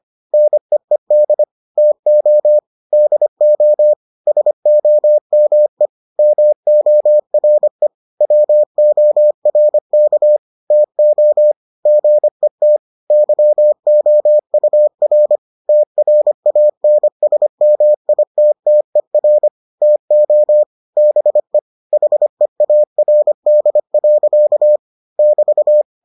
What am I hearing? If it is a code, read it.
Morse code, heard as a keyed tone, about 25 wpm, 600 Hz. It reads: NEED TO DO SOME MORE WORK TO GET YOUR TRANSMITTER TO BE HEARD. <BT>